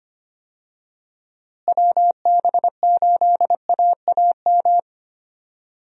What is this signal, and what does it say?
Morse code, heard as a keyed tone, about 25 wpm, 700 Hz. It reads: WB8AAM
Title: USA Call Signs - 6 Characters - Repeated 2x - 25wpm